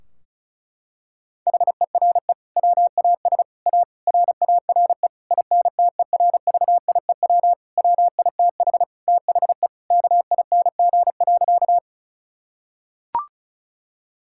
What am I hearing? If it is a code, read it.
Morse code, heard as a keyed tone, about 34 wpm, 700 Hz. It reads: HERE WAS A RARE INTERVIEW WITH THE KING.
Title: Sentences from Top 1600 Words 35wpm